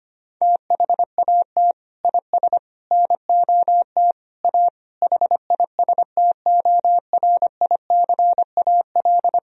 Morse code, heard as a keyed tone, about 25 wpm, 700 Hz. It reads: THAT IS NOT A HISTORICAL
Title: Sentences from Top 1000 Words - Review Part 4 - 25wpm